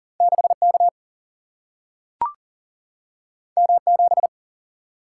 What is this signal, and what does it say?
Morse code, heard as a keyed tone, about 40 wpm, 700 Hz. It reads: 6 K M 7
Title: Single-Letter Single-Number 40wpm